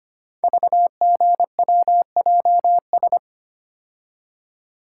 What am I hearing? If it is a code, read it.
Morse code, heard as a keyed tone, about 25 wpm, 700 Hz. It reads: VGWJS
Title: Code Groups - 5 Letters - 25wpm